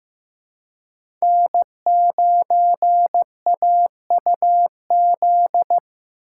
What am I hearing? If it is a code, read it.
Morse code, heard as a keyed tone, about 15 wpm, 700 Hz. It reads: N9AUZ
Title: USA Call Signs - 5 Characters - 4x Character Spacing - 15wpm